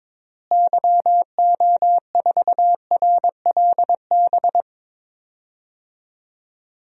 Morse code, heard as a keyed tone, about 22 wpm, 700 Hz. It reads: YO4RLB
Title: DXCC-100 Call Signs - 6 Characters - Repeated 2x - 22wpm